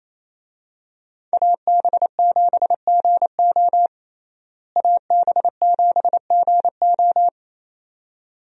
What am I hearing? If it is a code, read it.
Morse code, heard as a keyed tone, about 28 wpm, 700 Hz. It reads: AB7GO AB7GO
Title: USA Call Signs - 5 Characters - Repeated 2x - 28wpm